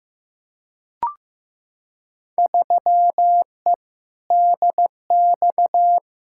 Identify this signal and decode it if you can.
Morse code, heard as a keyed tone, about 15 wpm, 700 Hz. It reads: 3E DX
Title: Field Day Exchanges Encore - 15wpm